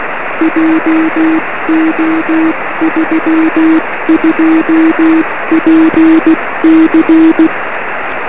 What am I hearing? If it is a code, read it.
Morse code, heard as a keyed tone, about 16 wpm, 320 Hz, with noise. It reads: JO32PC
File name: 1911z dj6ff 13cm